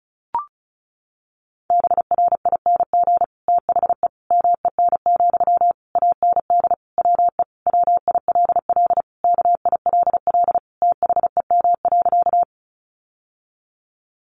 Morse code, heard as a keyed tone, about 34 wpm, 700 Hz. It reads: BRING THE MEN, AND WE WILL KILL THEM.
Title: Sentences from Top 1400 Words 35wpm